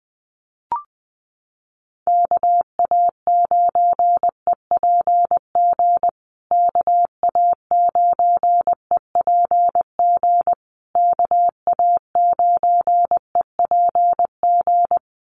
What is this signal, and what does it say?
Morse code, heard as a keyed tone, about 20 wpm, 700 Hz. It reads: KA9EPG KA9EPG KA9EPG
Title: USA Call Signs Repeated 3x 20wpm